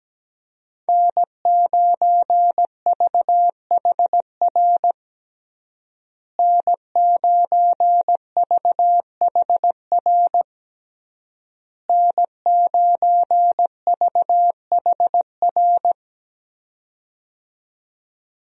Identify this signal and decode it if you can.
Morse code, heard as a keyed tone, about 17 wpm, 700 Hz. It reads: N9VHR N9VHR N9VHR
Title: USA Call Signs - 5 Characters - Repeated 3x - 17wpm